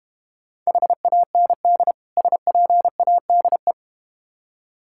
Morse code, heard as a keyed tone, about 31 wpm, 700 Hz. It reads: HAND SPADE
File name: Easy Words - Speed-Racing with Hints - 25wpm